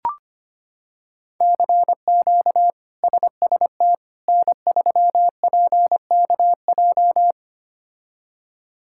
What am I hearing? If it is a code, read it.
Morse code, heard as a keyed tone, about 25 wpm, 700 Hz. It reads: CQ SST N3PKJ
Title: K1USN SST QSO Practice - 25wpm